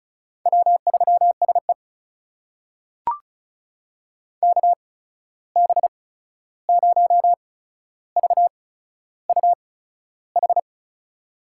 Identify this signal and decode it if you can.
Morse code, heard as a keyed tone, about 36 wpm, spaced as for 13 wpm, 700 Hz. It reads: W3SE KB0VUH
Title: USA Call Signs - 8x Character Spacing - 35wpm